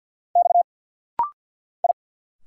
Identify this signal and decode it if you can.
Morse code, heard as a keyed tone, about 50 wpm, 700 Hz. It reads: X I
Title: Single Letter-Number - Rapid-Fire 50wpm